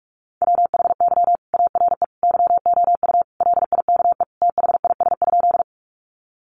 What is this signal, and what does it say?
Morse code, heard as a keyed tone, about 44 wpm, 700 Hz. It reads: WHY ARE YOU LIKE THIS?
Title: Sentences from Top 200 Words 45wpm